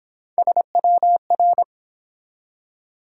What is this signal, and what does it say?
Morse code, heard as a keyed tone, about 23 wpm, 700 Hz. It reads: SWR